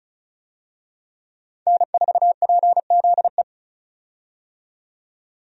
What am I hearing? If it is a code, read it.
Morse code, heard as a keyed tone, about 36 wpm, 700 Hz. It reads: N4PZE
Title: USA Call Signs - 5 Characters - 35wpm